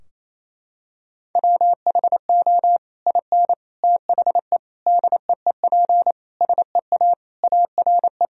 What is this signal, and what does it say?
Morse code, heard as a keyed tone, about 28 wpm, 700 Hz. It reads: WHO IN THE DEEP SEA ARE
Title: Sentences from Top 1000 Words - Review Part 3 - 28wpm